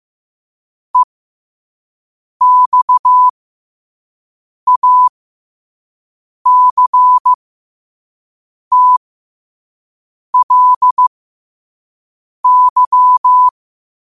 Morse code, heard as a keyed tone, about 15 wpm, spaced as for 5 wpm, 1000 Hz. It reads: EXACTLY